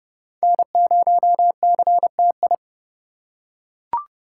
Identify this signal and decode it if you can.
Morse code, heard as a keyed tone, about 29 wpm, 700 Hz. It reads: N0CTI